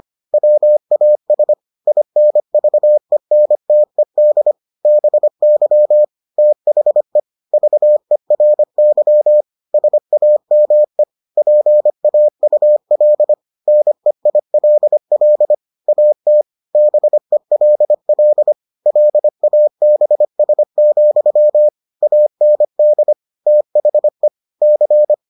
Morse code, heard as a keyed tone, about 25 wpm, 600 Hz. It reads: WAS INVENTED BY THE VERY SAME PAUL NEILL AT BELL LABS, AND THE C